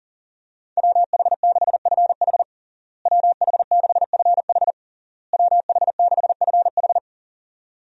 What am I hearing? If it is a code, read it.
Morse code, heard as a keyed tone, about 40 wpm, 700 Hz. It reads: WH6FH WH6FH WH6FH
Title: USA Call Signs - 5 Characters - Repeated 3x - 40wpm